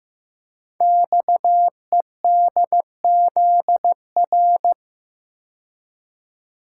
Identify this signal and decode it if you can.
Morse code, heard as a keyed tone, about 15 wpm, 700 Hz. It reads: XEDZR
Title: Code Groups - 5 Letters - 15wpm